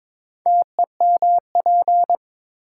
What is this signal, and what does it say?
Morse code, heard as a keyed tone, about 22 wpm, 700 Hz. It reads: TEMP
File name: Top 90 QSO Elements - 8x Character Spacing - 22wpm